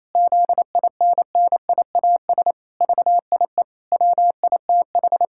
Morse code, heard as a keyed tone, about 27 wpm, 700 Hz. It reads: ZINNIAS VIE WITH
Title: Pangram Challenge v2 - 28wpm